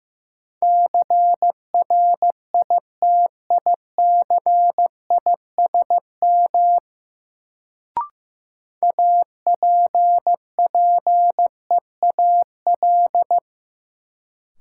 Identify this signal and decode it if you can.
Morse code, heard as a keyed tone, about 15 wpm, 700 Hz. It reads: CRITICISM APPEAL
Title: Sentences from Top 1300 Words - New Word Drill - 15wpm